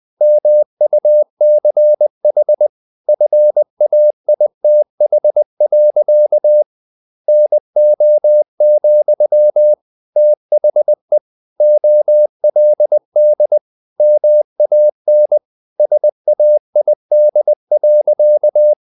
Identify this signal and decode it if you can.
Morse code, heard as a keyed tone, about 20 wpm, 600 Hz. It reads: MUCH FAITH. NO, THE OLD MAN SAID.
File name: the old man and the sea / the old man and the sea part 1 20wpm